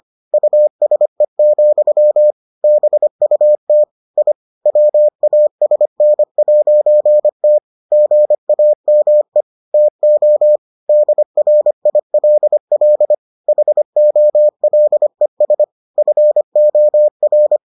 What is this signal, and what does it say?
Morse code, heard as a keyed tone, about 25 wpm, 600 Hz. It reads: USE, BUT I WASN'T GAME TO DRILL HOLES FOR